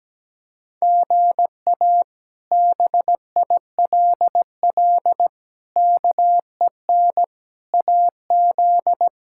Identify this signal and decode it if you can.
Morse code, heard as a keyed tone, about 17 wpm, 700 Hz. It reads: GA BILL KEN AZ
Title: K1USN SST QSO Practice - 17wpm